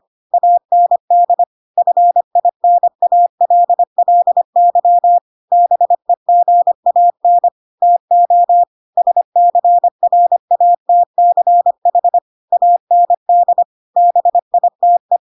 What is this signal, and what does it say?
Morse code, heard as a keyed tone, about 25 wpm, 700 Hz. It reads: AND FINALLY BEGAN TO SCRATCH AND BITE
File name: Pinocchio / Pinocchio ch2 25 25 ik1hge